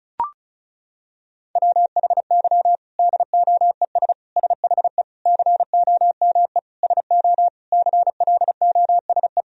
Morse code, heard as a keyed tone, about 34 wpm, 700 Hz. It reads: WHY DOES SHE COME SO CLOSE
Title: Sentences from Top 1000 Words - Review Part 1 - 35wpm